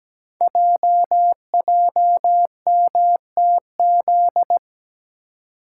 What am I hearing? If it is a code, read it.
Morse code, heard as a keyed tone, about 17 wpm, 700 Hz. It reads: JJMTZ
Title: Code Groups - 5 Letters - 17wpm